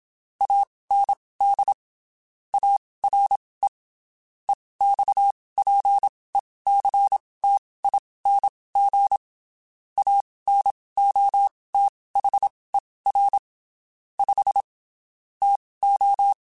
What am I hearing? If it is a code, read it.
Morse code, heard as a keyed tone, about 27 wpm, spaced as for 19 wpm, 800 Hz. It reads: AND ARE EXPECTING ANOTHER 5 TO
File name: (spaced) / QSOs Spaced-mp3s / WA7QOX 1 @25 sp